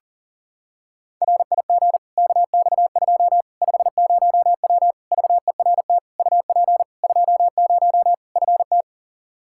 Morse code, heard as a keyed tone, about 40 wpm, 700 Hz. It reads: RIG KX2 50W VERT UP 20 FT
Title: Practice QSO - Exchange 2 - Info - 40wpm